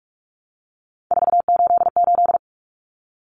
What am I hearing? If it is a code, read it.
Morse code, heard as a keyed tone, about 44 wpm, 700 Hz. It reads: 488
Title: Common Cut Numbers - 3-Digit 45wpm